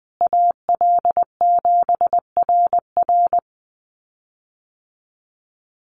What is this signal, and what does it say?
Morse code, heard as a keyed tone, about 20 wpm, 700 Hz. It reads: AL7RR
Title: USA Call Signs Repeated 3x 20wpm